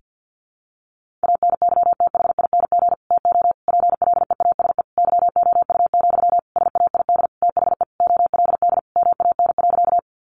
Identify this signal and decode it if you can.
Morse code, heard as a keyed tone, about 50 wpm, 700 Hz. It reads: ANYTHING TO PLEASE YOU, SAID THE OLD MAN.
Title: Sentences from Top 1500 Words 50wpm